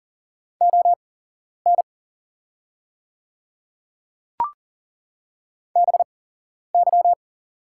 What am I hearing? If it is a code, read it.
Morse code, heard as a keyed tone, about 39 wpm, spaced as for 11 wpm, 700 Hz. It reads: ON BY
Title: 8x Character Spacing - Top 100 Words - 40wpm